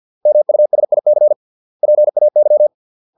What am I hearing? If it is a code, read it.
Morse code, heard as a keyed tone, about 50 wpm, 600 Hz. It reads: MUSIC PAY